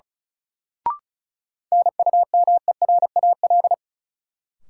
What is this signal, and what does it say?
Morse code, heard as a keyed tone, about 36 wpm, 700 Hz. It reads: NUMERAL
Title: Sentences from Top 1000 Words - Review Part 2 - New Word Drill - 35wpm